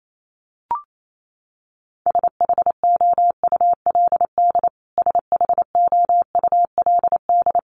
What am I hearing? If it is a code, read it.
Morse code, heard as a keyed tone, about 28 wpm, 700 Hz. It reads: SHOULD SHOULD
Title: Top 500 Words Repeated 28wpm